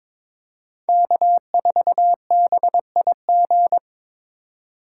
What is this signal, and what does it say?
Morse code, heard as a keyed tone, about 22 wpm, 700 Hz. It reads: K4BIG
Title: USA Call Signs - 5 Characters - 4x Character Spacing - 22wpm